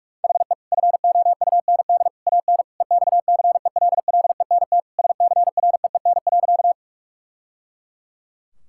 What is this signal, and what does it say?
Morse code, heard as a keyed tone, about 46 wpm, 700 Hz. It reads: HE FOUND AN EXCELLENT SCREEN.
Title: Sentences from Top 1000 Words - Review Part 6 - 45wpm